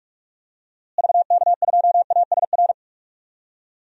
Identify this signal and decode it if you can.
Morse code, heard as a keyed tone, about 46 wpm, 700 Hz. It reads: VK2ASR
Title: DXCC-100 Call Signs - 6 Characters - Repeated 2x - 45wpm